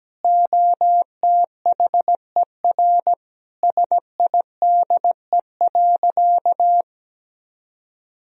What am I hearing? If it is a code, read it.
Morse code, heard as a keyed tone, about 17 wpm, 700 Hz. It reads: OTHER SIDE.
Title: Sentences from Top 1000 Words - Review Part 1 - 17wpm